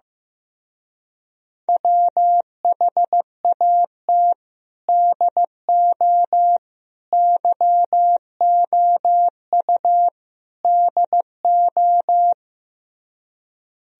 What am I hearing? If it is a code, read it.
Morse code, heard as a keyed tone, about 15 wpm, 700 Hz. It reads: WHAT DO YOU DO
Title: Speed-Racing - Sets of 4 Words 15wpm